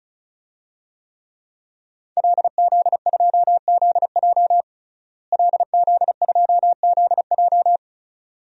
Morse code, heard as a keyed tone, about 36 wpm, 700 Hz. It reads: LZ2ZJ LZ2ZJ
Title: DXCC-100 Call Signs - 5 Characters - Repeated 2x - 35wpm